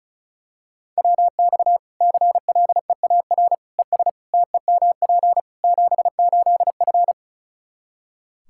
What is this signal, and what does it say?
Morse code, heard as a keyed tone, about 36 wpm, 700 Hz. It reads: WX CLEAR ES TEMP 78F
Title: Practice QSO - Exchange 2 - Info - 35wpm